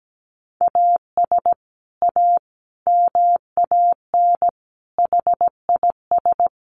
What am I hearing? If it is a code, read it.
Morse code, heard as a keyed tone, about 17 wpm, 700 Hz. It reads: AS A MAN HIS
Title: Sentences from Top 1200 Words 17wpm